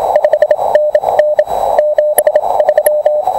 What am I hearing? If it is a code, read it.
Morse code, heard as a keyed tone, about 27 wpm, spaced as for 23 wpm, 600 Hz, with radio noise. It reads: 5NN 73